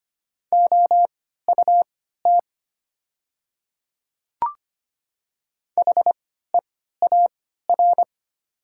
Morse code, heard as a keyed tone, about 25 wpm, spaced as for 14 wpm, 700 Hz. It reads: OUT HEAR
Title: 3x Character Spacing - Top 300 Words - 25wpm